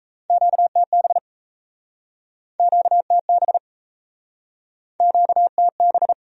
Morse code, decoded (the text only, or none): QTB QTB QTB